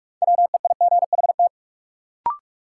Morse code, heard as a keyed tone, about 46 wpm, 700 Hz. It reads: WEIGHT
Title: Sentences from Top 800 Words - New Word Drill - 45wpm